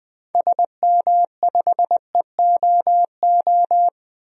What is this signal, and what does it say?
Morse code, heard as a keyed tone, about 21 wpm, 700 Hz. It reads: SM5EOO
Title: DXCC-100 Call Signs - 6 Characters - 20wpm